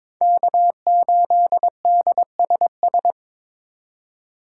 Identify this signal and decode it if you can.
Morse code, heard as a keyed tone, about 21 wpm, 700 Hz. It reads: K8DSS
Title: K1USN SST QSO Practice - 22wpm